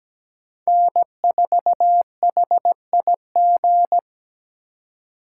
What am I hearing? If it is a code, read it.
Morse code, heard as a keyed tone, about 20 wpm, 700 Hz. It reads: N4HIG
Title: Speed Racing - USA Call Signs 17wpm